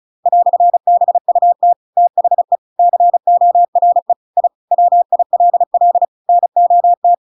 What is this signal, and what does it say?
Morse code, heard as a keyed tone, about 35 wpm, 700 Hz. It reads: "BUT THE CORE I WILL NOT
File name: Pinocchio / Pinocchio ch7 35 35 ik1hge